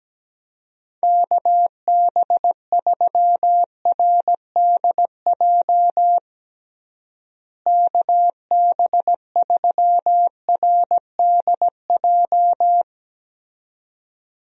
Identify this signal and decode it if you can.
Morse code, heard as a keyed tone, about 17 wpm, 700 Hz. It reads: KB3RDJ KB3RDJ
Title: USA Call Signs - 6 Characters - Repeated 2x - 17wpm